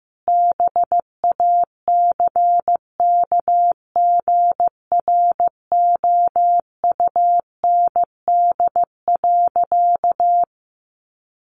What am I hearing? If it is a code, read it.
Morse code, heard as a keyed tone, about 15 wpm, 700 Hz. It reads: BACKGROUND.
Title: Sentences from Top 1300 Words 15wpm